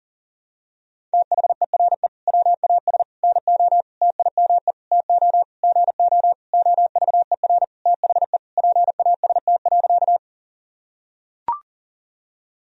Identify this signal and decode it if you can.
Morse code, heard as a keyed tone, about 40 wpm, 700 Hz. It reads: THERE WAS NO TIME TO GO OVER THE PAST.
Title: Sentences from Top 1000 Words - Review Part 3 - 40wpm